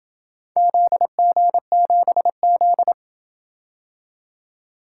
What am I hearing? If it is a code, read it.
Morse code, heard as a keyed tone, about 26 wpm, 700 Hz. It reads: ZG7Z